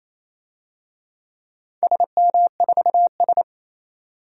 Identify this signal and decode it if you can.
Morse code, heard as a keyed tone, about 28 wpm, 700 Hz. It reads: SM4S
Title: DXCC-100 Call Signs - 4 Characters - 28wpm